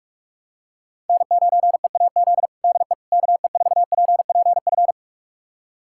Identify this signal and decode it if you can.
Morse code, heard as a keyed tone, about 41 wpm, 700 Hz. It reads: N9EAZ DE KE4PPF
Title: USA Callsign DE Callsign 45wpm